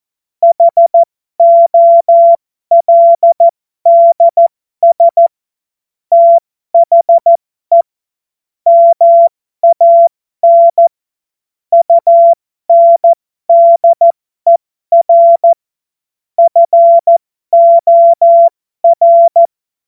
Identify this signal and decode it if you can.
Morse code, heard as a keyed tone, about 14 wpm, spaced as for 12 wpm, 680 Hz. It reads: HOLDS THE MAN UNDER FOR